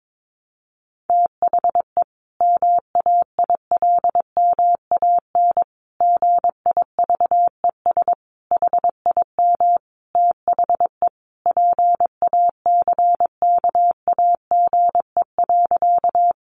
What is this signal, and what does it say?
Morse code, heard as a keyed tone, about 22 wpm, 700 Hz. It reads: THE MAILMAN GIVES HIM THE PACKAGE.
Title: Additional Easy Phrases 22wpm